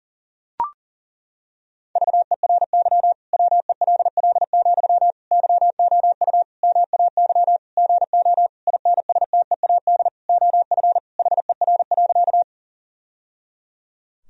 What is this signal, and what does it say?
Morse code, heard as a keyed tone, about 40 wpm, 700 Hz. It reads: VERY WELL, YOU MAY GO INSTEAD OF HER.
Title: Sentences from Top 1000 Words - Review Part 4 - 40wpm